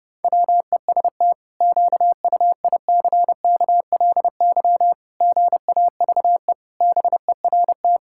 Morse code, heard as a keyed tone, about 32 wpm, 700 Hz. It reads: WEST QUICKLY GAVE BERT
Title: Pangram Challenge v2 - 30wpm